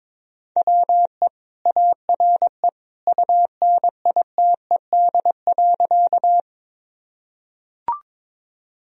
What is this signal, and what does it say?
Morse code, heard as a keyed tone, about 21 wpm, 700 Hz. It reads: WE ARE UNITED.
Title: Sentences from Top 1000 Words - Review Part 4 - 22wpm